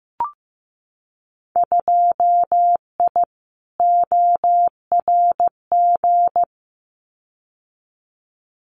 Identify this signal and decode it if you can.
Morse code, heard as a keyed tone, about 15 wpm, 700 Hz. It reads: 2I ORG
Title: Winter Field Day Exchanges Encore 15wpm